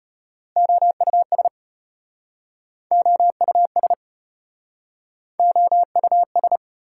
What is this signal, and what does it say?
Morse code, heard as a keyed tone, about 33 wpm, 700 Hz. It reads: OUS OUS OUS